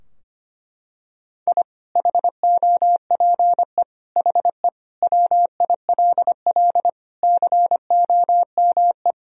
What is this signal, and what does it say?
Morse code, heard as a keyed tone, about 25 wpm, 700 Hz. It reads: I HOPE HE WILL COME
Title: Sentences from Top 1000 Words - Review Part 3 - 25wpm